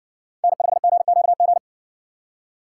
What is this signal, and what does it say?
Morse code, heard as a keyed tone, about 59 wpm, 700 Hz. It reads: N5GQG